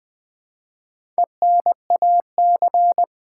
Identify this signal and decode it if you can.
Morse code, heard as a keyed tone, about 20 wpm, 700 Hz. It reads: ENAC
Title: Build A Word - 1 Letter At A Time - 5-Letter Words - 20wpm